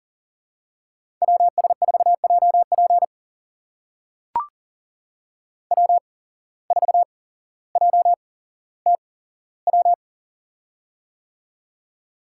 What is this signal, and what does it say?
Morse code, heard as a keyed tone, about 40 wpm, spaced as for 17 wpm, 700 Hz. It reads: WS4JP W4JTW
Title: USA Call Signs - 5 Characters - 8x Character Spacing - 40wpm